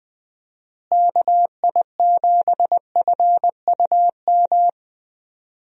Morse code, heard as a keyed tone, about 21 wpm, 700 Hz. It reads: KI7FUM
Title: USA Call Signs - 6 Characters - Repeated 2x - 20wpm